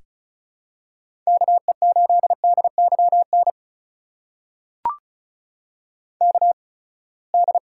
Morse code, heard as a keyed tone, about 34 wpm, spaced as for 24 wpm, 700 Hz. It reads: KE8DYN KD
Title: USA Call Signs - 6 Characters - 8x Character Spacing - 35wpm